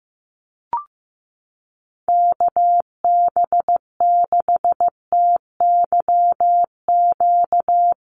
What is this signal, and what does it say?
Morse code, heard as a keyed tone, about 15 wpm, 700 Hz. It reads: KB6TYQ
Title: USA Call Signs Repeated 2x 15wpm